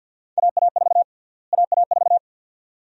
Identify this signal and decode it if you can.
Morse code, heard as a keyed tone, about 52 wpm, 700 Hz. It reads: AA4 AA4
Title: USA Call Sign Prefixes - Repeated 2x - 50wpm